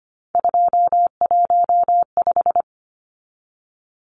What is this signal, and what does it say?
Morse code, heard as a keyed tone, about 25 wpm, 700 Hz. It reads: 215
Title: Numbers - 3-Digit 25wpm